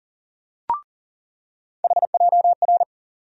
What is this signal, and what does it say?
Morse code, heard as a keyed tone, about 38 wpm, 700 Hz. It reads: HJR
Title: USA Call Sign Postfixes - 3 Letters 40wpm